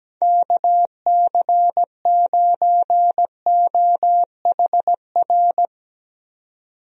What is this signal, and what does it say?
Morse code, heard as a keyed tone, about 17 wpm, 700 Hz. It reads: KC9OHR
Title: USA Call Signs - 6 Characters - 17wpm